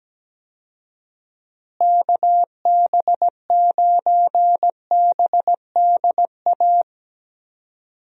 Morse code, heard as a keyed tone, about 17 wpm, 700 Hz. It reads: KB9BDA
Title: USA Call Signs - 6 Characters - Repeated 2x - 17wpm